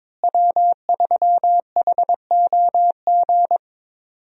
Morse code, heard as a keyed tone, about 22 wpm, 690 Hz. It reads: W3HOG
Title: USA Call Signs - 5 Characters - Repeated 3x - 22wpm